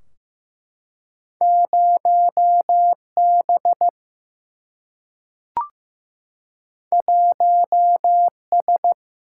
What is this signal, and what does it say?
Morse code, heard as a keyed tone, about 15 wpm, 700 Hz. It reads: 0B 1S
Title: Single-Letter Single-Number 15wpm